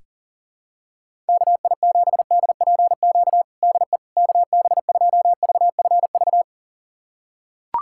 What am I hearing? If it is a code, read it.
Morse code, heard as a keyed tone, about 40 wpm, 700 Hz. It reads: KI7DPQ DE KB2VFV
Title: USA Callsign DE Callsign - 6 Characters - 40wpm